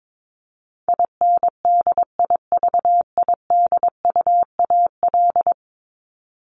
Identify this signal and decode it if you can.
Morse code, heard as a keyed tone, about 22 wpm, 700 Hz. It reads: INDIVIDUAL
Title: ES:S Suffix - Top 100 Words with Base Words 22wpm